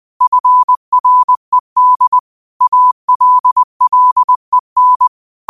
Morse code, heard as a keyed tone, about 20 wpm, 1000 Hz. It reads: FRED ALLEN
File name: qotdfiles / 20wpm